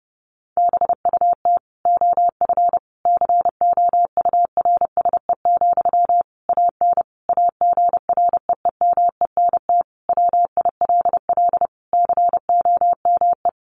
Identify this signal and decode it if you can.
Morse code, heard as a keyed tone, about 30 wpm, 700 Hz. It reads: BUT OF COURSE, AN AGREEMENT WILL COME
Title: Sentences from Top 1500 Words 30wpm